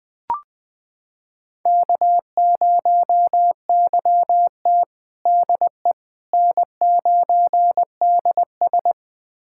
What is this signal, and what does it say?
Morse code, heard as a keyed tone, about 20 wpm, 700 Hz. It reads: K0YT DE N9DS
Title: USA Callsign DE Callsign - 4 Characters - 20wpm